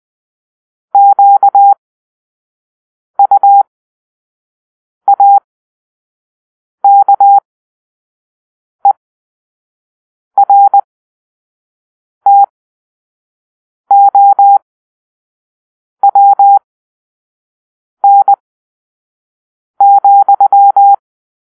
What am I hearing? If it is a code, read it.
Morse code, heard as a keyed tone, about 20 wpm, spaced as for 5 wpm, 800 Hz. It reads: QUAKERTOWN,
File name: arrl / Grp-6 Drill-3